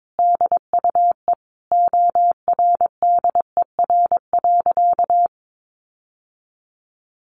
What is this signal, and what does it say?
Morse code, heard as a keyed tone, about 22 wpm, 700 Hz. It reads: DUE ORDER.
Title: Sentences from Top 900 Words 22wpm